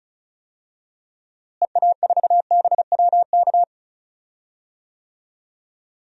Morse code, heard as a keyed tone, about 36 wpm, 700 Hz. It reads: EA4BWK
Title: DXCC-100 Call Signs - 6 Characters - Repeated 2x - 35wpm